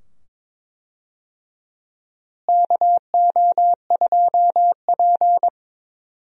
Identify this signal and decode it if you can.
Morse code, heard as a keyed tone, about 22 wpm, 700 Hz. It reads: KO2P
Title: USA Call Signs - 4 Characters - Repeated 2x - 22wpm